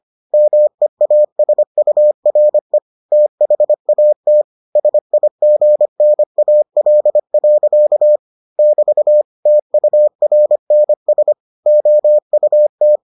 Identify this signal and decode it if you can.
Morse code, heard as a keyed tone, about 25 wpm, 600 Hz. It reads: MEASURE THAT SIGNAL. <BT> TURNS OUT